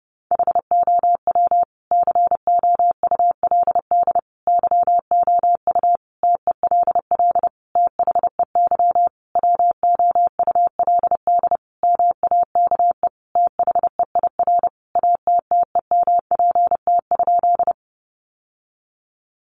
Morse code, heard as a keyed tone, about 30 wpm, 700 Hz. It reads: HOW COULD YOU TELL THEY WOULD MAKE THEIR ATTEMPT?
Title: Sentences from Top 1000 Words 30wpm